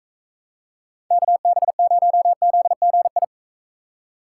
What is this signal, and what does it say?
Morse code, heard as a keyed tone, about 43 wpm, 700 Hz. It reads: KB0ZGI